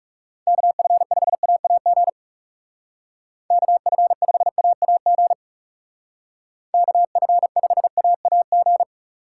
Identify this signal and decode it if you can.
Morse code, heard as a keyed tone, about 35 wpm, 700 Hz. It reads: KF5AAG KF5AAG KF5AAG
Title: Speed Racing - USA 2x3 Call Signs 30wpm